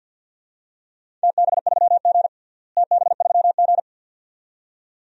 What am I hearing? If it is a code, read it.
Morse code, heard as a keyed tone, about 52 wpm, 700 Hz. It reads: TB3G TB3G
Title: DXCC-100 Call Signs - 4 Characters - Repeated 2x - 50wpm